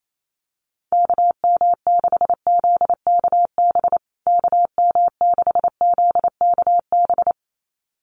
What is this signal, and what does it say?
Morse code, heard as a keyed tone, about 28 wpm, 700 Hz. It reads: KM6ZKB KM6ZKB
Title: USA Call Signs Repeated 2x 28wpm